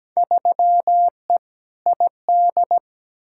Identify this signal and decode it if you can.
Morse code, heard as a keyed tone, about 17 wpm, 700 Hz. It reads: 3E ID
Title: Field Day Exchanges Encore - 17wpm